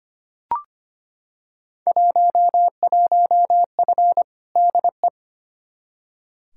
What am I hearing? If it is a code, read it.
Morse code, heard as a keyed tone, about 25 wpm, 700 Hz. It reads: 11F DE
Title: Field Day Exchanges Encore - 25wpm